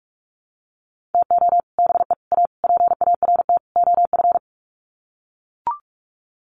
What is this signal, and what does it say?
Morse code, heard as a keyed tone, about 44 wpm, 700 Hz. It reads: TO BE A PART OF
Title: Sets of 5 Words 45wpm